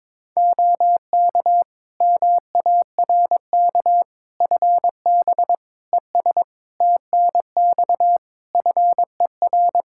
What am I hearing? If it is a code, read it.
Morse code, heard as a keyed tone, about 22 wpm, 700 Hz. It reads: OK MARK FB ES TNX FER
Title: Practice QSO - Exchange 2 - Info - 22wpm